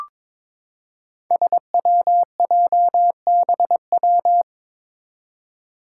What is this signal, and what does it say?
Morse code, heard as a keyed tone, about 22 wpm, 700 Hz. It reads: SWJBW
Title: Five Challenging Letter-Number Combinations 22wpm